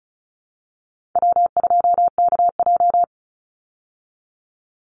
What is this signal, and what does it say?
Morse code, heard as a keyed tone, about 34 wpm, 700 Hz. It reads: W2KJ
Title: Call Signs Repeated 3x - 35wpm